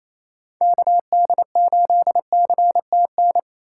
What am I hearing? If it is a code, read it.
Morse code, heard as a keyed tone, about 27 wpm, 700 Hz. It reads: KD8CTN
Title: USA Call Signs - 6 Characters - 4x Character Spacing - 28wpm